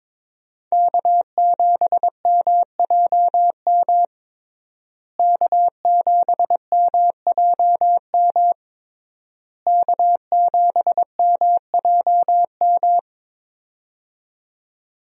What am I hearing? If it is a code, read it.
Morse code, heard as a keyed tone, about 22 wpm, 700 Hz. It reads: K7MJM K7MJM K7MJM
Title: USA Call Signs - 5 Characters - Repeated 3x - 22wpm